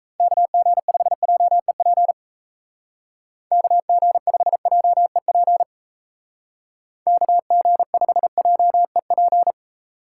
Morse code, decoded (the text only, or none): KG5JEP KG5JEP KG5JEP